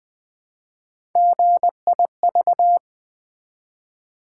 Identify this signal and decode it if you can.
Morse code, heard as a keyed tone, about 20 wpm, 700 Hz. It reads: GIV
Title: Build A Word - 1 Letter At A Time - 4-Letter Words - 20wpm